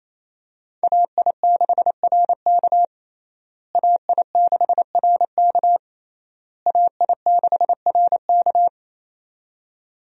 Morse code, heard as a keyed tone, about 28 wpm, 700 Hz. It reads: AI6RK AI6RK AI6RK
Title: USA Call Signs - 5 Characters - Repeated 3x - 28wpm